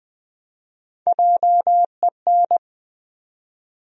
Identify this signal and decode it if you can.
Morse code, heard as a keyed tone, about 20 wpm, 700 Hz. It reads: JEN
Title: USA Common Names - 20wpm